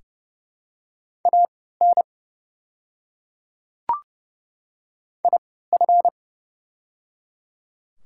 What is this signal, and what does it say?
Morse code, heard as a keyed tone, about 30 wpm, spaced as for 17 wpm, 700 Hz. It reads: AN IF
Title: 3x Character Spacing - Top 100 Words - 30wpm